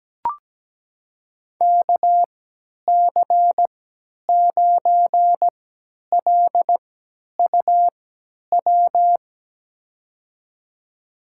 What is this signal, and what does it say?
Morse code, heard as a keyed tone, about 17 wpm, spaced as for 10 wpm, 700 Hz. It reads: KC9LUW
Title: USA Call Signs - 6 Characters - 3x Character Spacing - 17wpm